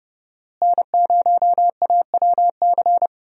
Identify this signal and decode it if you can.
Morse code, heard as a keyed tone, about 30 wpm, 700 Hz. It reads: N0AWC
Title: USA Call Signs - 5 Characters - Repeated 2x - 30wpm